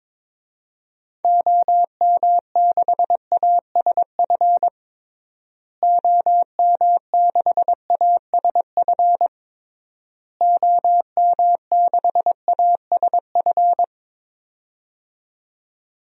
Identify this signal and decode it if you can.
Morse code, heard as a keyed tone, about 22 wpm, 700 Hz. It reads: OM6ASF OM6ASF OM6ASF
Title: DXCC-100 Call Signs - 6 Characters - Repeated 3x - 22wpm